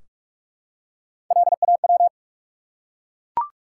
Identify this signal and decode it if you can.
Morse code, heard as a keyed tone, about 44 wpm, 700 Hz. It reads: LAW
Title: Three Letter Words - 45wpm